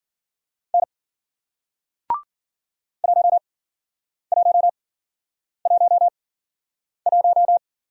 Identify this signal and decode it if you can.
Morse code, heard as a keyed tone, about 45 wpm, spaced as for 9 wpm, 700 Hz. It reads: N 1111
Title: Speed-Racing Single Letter-Number 40wpm